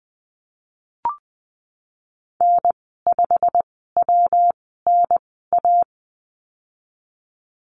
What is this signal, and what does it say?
Morse code, heard as a keyed tone, about 20 wpm, spaced as for 15 wpm, 700 Hz. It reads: N5WNA
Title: 147 Lesson 24g - Letter y - Call Signs 20wpm